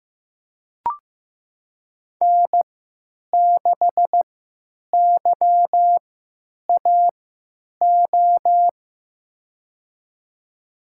Morse code, heard as a keyed tone, about 15 wpm, spaced as for 9 wpm, 700 Hz. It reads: N6YAO
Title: USA Call Signs - 3x Character Spacing - 15wpm